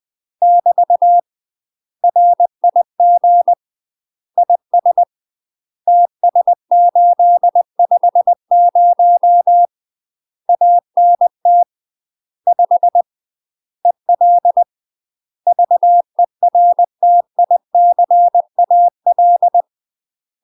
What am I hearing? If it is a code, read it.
Morse code, heard as a keyed tone, about 20 wpm, 700 Hz. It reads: <BT> RIG IS TS850 ANT 5 EL VERTICAL
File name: UK MASTER QSOs TRAINING / 20WPM / UK MASTER QSOs 0009